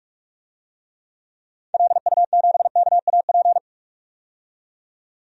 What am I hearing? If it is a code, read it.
Morse code, heard as a keyed tone, about 46 wpm, 700 Hz. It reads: LU7KAP
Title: DXCC-100 Call Signs - 6 Characters - 45wpm